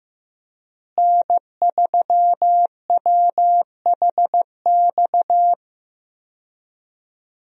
Morse code, heard as a keyed tone, about 15 wpm, 700 Hz. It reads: N3WHX
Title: USA Call Signs - 5 Characters - Repeated 3x - 15wpm